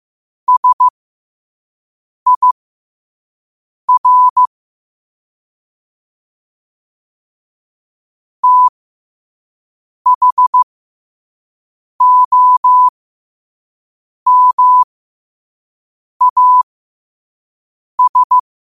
Morse code, heard as a keyed tone, about 15 wpm, spaced as for 5 wpm, 1000 Hz. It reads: SIR THOMAS